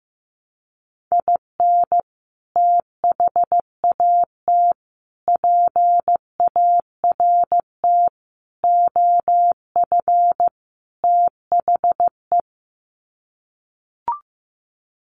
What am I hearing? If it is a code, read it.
Morse code, heard as a keyed tone, about 15 wpm, 700 Hz. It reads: IN THAT PART OF THE
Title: Sets of 5 Words Encore2 15wpm